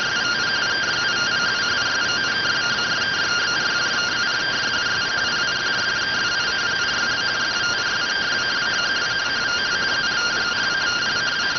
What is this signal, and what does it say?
Two-tone teleprinter signal, about 170 Hz shift.